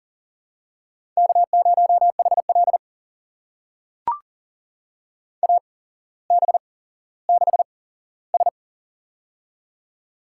Morse code, heard as a keyed tone, about 42 wpm, spaced as for 17 wpm, 690 Hz. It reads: K0HL AB6S
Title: USA Call Signs - 4 Characters - 8x Character Spacing - 40wpm